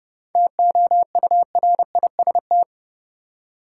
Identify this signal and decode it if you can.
Morse code, heard as a keyed tone, about 31 wpm, 700 Hz. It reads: TOURIST
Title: Seven Letter Words - 30wpm